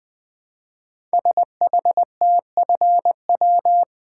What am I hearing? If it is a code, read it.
Morse code, heard as a keyed tone, about 20 wpm, 700 Hz. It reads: SHTFW
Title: Code Groups - 5 Letters - 20wpm